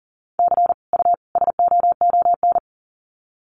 Morse code, heard as a keyed tone, about 40 wpm, 700 Hz. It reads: C U SOON
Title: Common QSO Phrases 40wpm